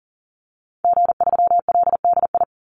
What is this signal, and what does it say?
Morse code, heard as a keyed tone, about 40 wpm, 700 Hz. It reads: G3LDI
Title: Call Signs Repeated 2x - 40wpm